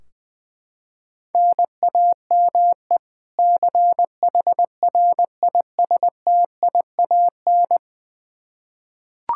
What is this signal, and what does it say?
Morse code, heard as a keyed tone, about 21 wpm, 700 Hz. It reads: NAME CHRISTIAN
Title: Practice QSO - Exchange 1 - Beginning - 20wpm